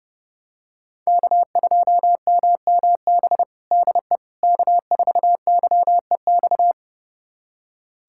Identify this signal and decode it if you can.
Morse code, heard as a keyed tone, about 32 wpm, 700 Hz. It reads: K2MMB DE K4YEX
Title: USA Callsign DE Callsign - 5 Characters - 30wpm